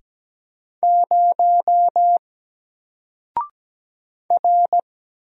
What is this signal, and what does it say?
Morse code, heard as a keyed tone, about 17 wpm, 700 Hz. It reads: 0 R
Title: Single Letter-Number 17wpm